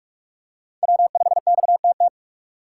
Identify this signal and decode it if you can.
Morse code, heard as a keyed tone, about 47 wpm, 700 Hz. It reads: W5XTT
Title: USA Call Signs - 5 Characters - 4x Character Spacing - 45wpm